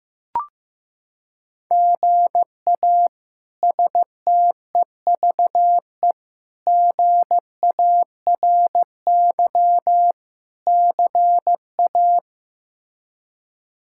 Morse code, heard as a keyed tone, about 15 wpm, 700 Hz. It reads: GA STEVE GARY CA
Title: K1USN SST QSO Practice - 15wpm